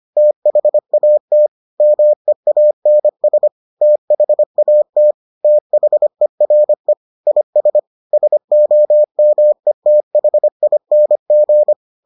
Morse code, heard as a keyed tone, about 25 wpm, 600 Hz. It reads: THAT MEANS THAT THERE IS SOMETHING